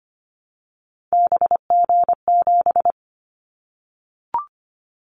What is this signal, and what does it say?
Morse code, heard as a keyed tone, about 25 wpm, 700 Hz. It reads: BG7
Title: Callsign Prefixes for Top 100 DXCC - 25wpm